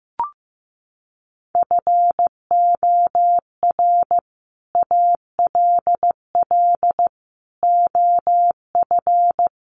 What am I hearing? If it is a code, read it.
Morse code, heard as a keyed tone, about 15 wpm, 700 Hz. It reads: FOR ALL OF
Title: Sets of 5 Words 15wpm